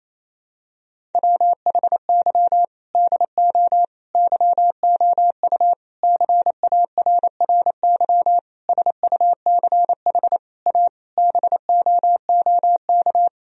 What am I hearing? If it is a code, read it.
Morse code, heard as a keyed tone, about 28 wpm, 700 Hz. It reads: WHY DO YOU CARRY SUCH A BOOK
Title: Sentences from Top 1000 Words - Review Part 1 - 28wpm